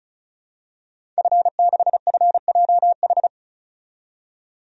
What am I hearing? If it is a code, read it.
Morse code, heard as a keyed tone, about 36 wpm, 700 Hz. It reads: F6FJH
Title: DXCC-100 Call Signs - 5 Characters - 35wpm